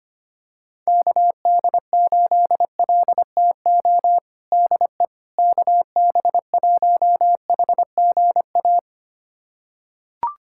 Morse code, heard as a keyed tone, about 25 wpm, 700 Hz. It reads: KD8LTO DE KB1HGA
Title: USA Callsign DE Callsign - 6 Characters - 25wpm